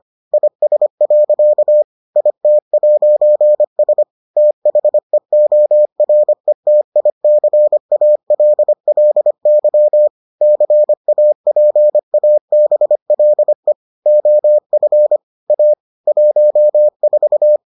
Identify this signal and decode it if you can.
Morse code, heard as a keyed tone, about 25 wpm, 600 Hz. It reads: IS. IT'S THEORETICALLY CAPABLE OF A 14